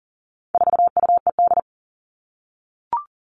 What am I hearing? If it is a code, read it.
Morse code, heard as a keyed tone, about 40 wpm, 700 Hz. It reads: 4UED
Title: All Cut Numbers - 4-Digit 40wpm